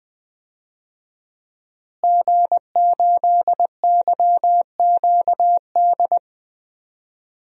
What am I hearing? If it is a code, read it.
Morse code, heard as a keyed tone, about 20 wpm, 700 Hz. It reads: G8YQD
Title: DXCC-100 Call Signs - 5 Characters - Repeated 2x - 20wpm